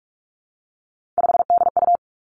Morse code, heard as a keyed tone, about 46 wpm, 700 Hz. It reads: 5DU
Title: Three Challenging Letter-Number Combinations 45wpm